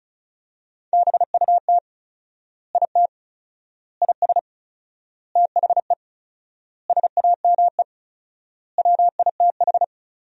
Morse code, heard as a keyed tone, about 36 wpm, 700 Hz. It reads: BUT IT IS THE SAME WITH
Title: 4x Word-Spacing - Sentences from Top 300 Words - 35wpm